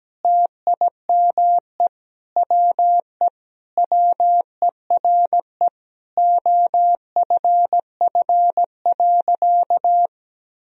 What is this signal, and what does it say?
Morse code, heard as a keyed tone, about 17 wpm, 700 Hz. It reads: TIME WE WERE OFF.
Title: Sentences from Top 1000 Words - Review Part 1 - 17wpm